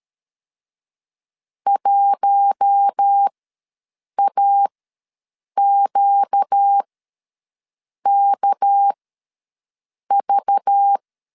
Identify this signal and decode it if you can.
Morse code, heard as a keyed tone, about 13 wpm, spaced as for 6 wpm, 800 Hz. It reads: 1AQKV
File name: LEZIONI / 6 PKX0 / 6 PKX0 ws 12wpm 3448